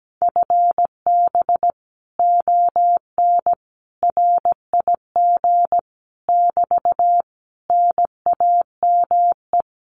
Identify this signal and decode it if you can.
Morse code, heard as a keyed tone, about 17 wpm, 700 Hz. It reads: FB ON RIG <BT> NAME
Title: Common QSO Phrases Encore 17wpm